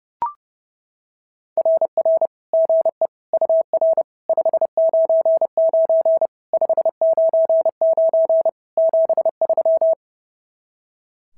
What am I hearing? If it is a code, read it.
Morse code, heard as a keyed tone, about 26 wpm, 650 Hz. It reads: RR GE UR 599 599 73
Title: SOTA QSO Practice 30wpm